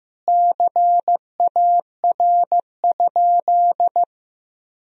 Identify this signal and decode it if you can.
Morse code, heard as a keyed tone, about 15 wpm, 700 Hz. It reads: CAR?
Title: Sentences from Top 300 Words and Abbreviations 15wpm